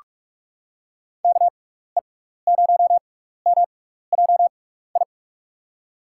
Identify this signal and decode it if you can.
Morse code, heard as a keyed tone, about 46 wpm, spaced as for 16 wpm, 700 Hz. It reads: KE0MJI
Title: USA Call Signs - 6 Characters - 6x Character Spacing - 45wpm